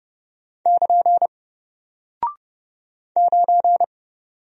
Morse code, heard as a keyed tone, about 30 wpm, 700 Hz. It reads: <KN> 9
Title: Single Character (Letter:Number:Punctuation:Prosign) Alt 30wpm